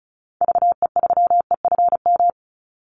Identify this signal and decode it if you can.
Morse code, heard as a keyed tone, about 34 wpm, 700 Hz. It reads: VE3EFM
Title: Call Signs - 35wpm